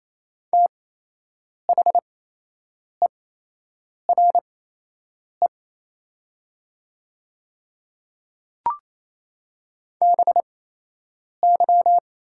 Morse code, heard as a keyed tone, about 28 wpm, spaced as for 8 wpm, 690 Hz. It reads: THERE BY
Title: 8x Character Spacing - Top 300 Words - 28wpm